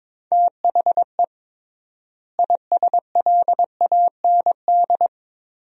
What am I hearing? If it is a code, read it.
Morse code, heard as a keyed tone, about 22 wpm, 700 Hz. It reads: THE ISLAND
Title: 3x Word-Spacing - Sentences from Top 500 Words - 22wpm